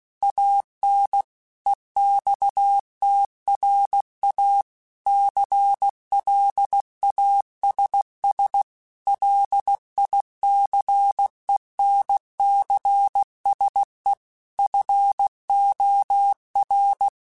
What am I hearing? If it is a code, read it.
Morse code, heard as a keyed tone, about 16 wpm, 800 Hz. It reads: AN EXTRA CLASS LICENCSE FOR
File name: QSOs / QSOs-mp3s / kc7sac@15